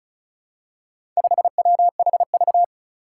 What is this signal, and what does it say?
Morse code, heard as a keyed tone, about 36 wpm, 700 Hz. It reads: 5WHV
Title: Four Challenging Letter-Number Combinations 35wpm